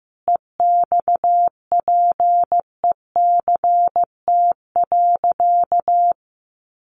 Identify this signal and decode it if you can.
Morse code, heard as a keyed tone, about 15 wpm, 700 Hz. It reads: EXPECT.
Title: Sentences from Top 1100 Words 15wpm